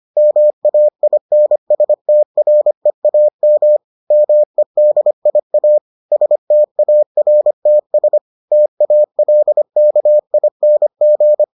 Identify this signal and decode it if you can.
Morse code, heard as a keyed tone, about 25 wpm, 600 Hz. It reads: MAINSTREAM MEDIA STARTS TALKING